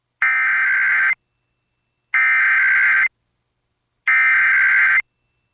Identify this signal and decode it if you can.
Emergency Alert System (SAME) tones (header bursts), heard as harsh buzzing bursts. It is heard from 0.2 to 5.0 s.